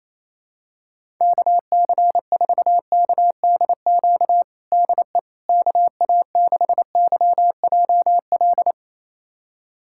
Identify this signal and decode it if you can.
Morse code, heard as a keyed tone, about 28 wpm, 700 Hz. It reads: KC4KDQ DE KA6YJL
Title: USA Callsign DE Callsign - 6 Characters - 28wpm